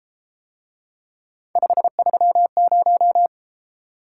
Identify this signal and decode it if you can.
Morse code, heard as a keyed tone, about 32 wpm, 700 Hz. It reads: 530